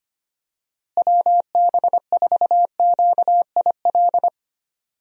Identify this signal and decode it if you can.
Morse code, heard as a keyed tone, about 25 wpm, 700 Hz. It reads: WB4QIL
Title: USA Call Signs - 6 Characters - Repeated 3x - 25wpm